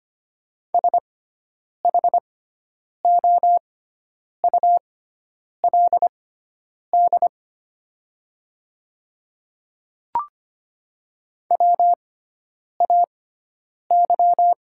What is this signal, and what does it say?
Morse code, heard as a keyed tone, about 25 wpm, spaced as for 9 wpm, 700 Hz. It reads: SHOULD WAY
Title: 6x Character Spacing - Top 500 Words - 25wpm